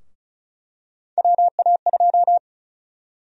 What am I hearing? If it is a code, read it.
Morse code, heard as a keyed tone, about 36 wpm, 700 Hz. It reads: WA2
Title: USA Call Sign Prefixes - 35wpm